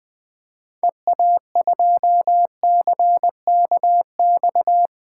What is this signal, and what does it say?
Morse code, heard as a keyed tone, about 20 wpm, 700 Hz. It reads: EA2CKX
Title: 2023 SOTA Chaser Call Signs - 20wpm